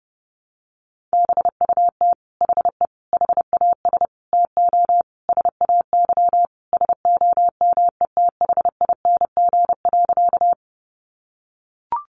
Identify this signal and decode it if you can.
Morse code, heard as a keyed tone, about 30 wpm, 700 Hz. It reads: BUT HE HAS TO SAY SOMETHING.
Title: Sentences from Top 600 Words 30wpm